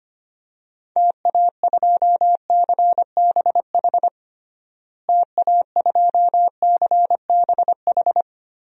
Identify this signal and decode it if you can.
Morse code, heard as a keyed tone, about 25 wpm, 700 Hz. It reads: TA2CBH TA2CBH
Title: DXCC-100 Call Signs - 6 Characters - Repeated 2x - 25wpm